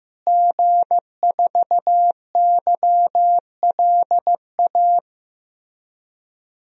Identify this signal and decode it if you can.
Morse code, heard as a keyed tone, about 16 wpm, 700 Hz. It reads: G4YLA
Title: DXCC-100 Call Signs - 5 Characters - Repeated 3x - 15wpm